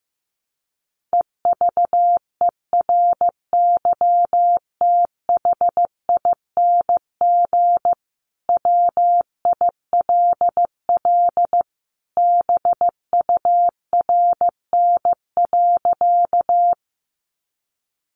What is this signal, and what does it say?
Morse code, heard as a keyed tone, about 15 wpm, 700 Hz. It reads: EVERYTHING WILL BURN.
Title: Sentences from Top 1300 Words 15wpm